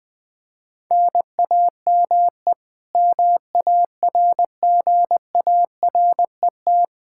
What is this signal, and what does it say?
Morse code, heard as a keyed tone, about 20 wpm, 700 Hz. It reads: NAME MARGARET
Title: K1USN SST QSO Practice 20wpm